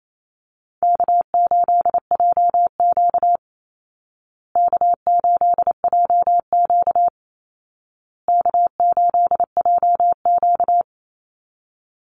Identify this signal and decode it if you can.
Morse code, heard as a keyed tone, about 28 wpm, 700 Hz. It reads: K8JQ K8JQ K8JQ
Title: Easy Call Signs Repeated 3x - 28wpm